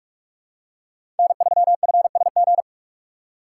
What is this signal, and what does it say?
Morse code, heard as a keyed tone, about 46 wpm, 700 Hz. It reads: N3FSG
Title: USA Call Signs - 5 Characters - 3x Character Spacing - 45wpm